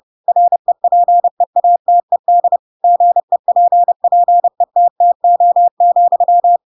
Morse code, heard as a keyed tone, about 30 wpm, 700 Hz. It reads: REPEATED GEPPETTO,
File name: Pinocchio / Pinocchio ch7 30 30 ik1hge